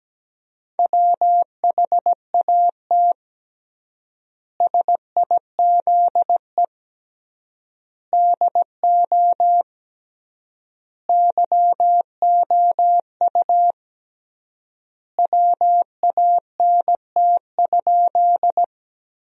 Morse code, heard as a keyed tone, about 17 wpm, 700 Hz. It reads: WHAT SIZE DO YOU WANT?
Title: 3x Word-Spacing - Sentences from Top 400 Words - 17wpm